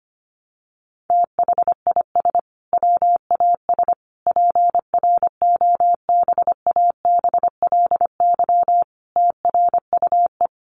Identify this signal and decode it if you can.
Morse code, heard as a keyed tone, about 25 wpm, 700 Hz. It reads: THIS WAS PROBABLY TRUE
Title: Sentences from Top 700 Words 25wpm